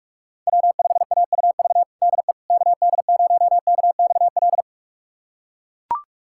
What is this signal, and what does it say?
Morse code, heard as a keyed tone, about 41 wpm, 700 Hz. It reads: W5AUV DE KD0KXL
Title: USA Callsign DE Callsign 45wpm